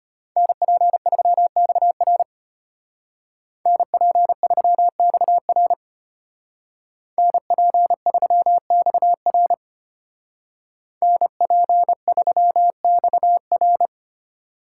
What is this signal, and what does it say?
Morse code, heard as a keyed tone, about 31 wpm, 700 Hz. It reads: NP3XR NP3XR NP3XR NP3XR